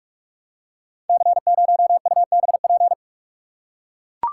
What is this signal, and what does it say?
Morse code, heard as a keyed tone, about 44 wpm, 700 Hz. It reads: C0UBP
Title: Code Groups - 5 Letter:Numbers - 45wpm